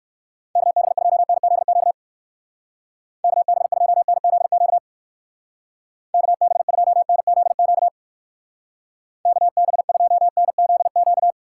Morse code, heard as a keyed tone, about 55 wpm, 700 Hz. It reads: KB2NZQ KB2NZQ KB2NZQ KB2NZQ
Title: Speed Racing - USA 2x3 Call Signs 45wpm